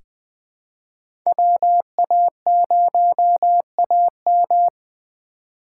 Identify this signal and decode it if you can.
Morse code, heard as a keyed tone, about 20 wpm, 700 Hz. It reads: WA0AM
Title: USA Call Signs - 5 Characters - 3x Character Spacing - 20wpm